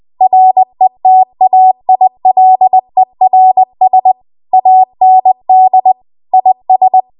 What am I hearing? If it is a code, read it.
Morse code, heard as a keyed tone, about 20 wpm, 750 Hz. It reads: RETAILERS AND IS
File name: Archive / 20 WPM / 20WPM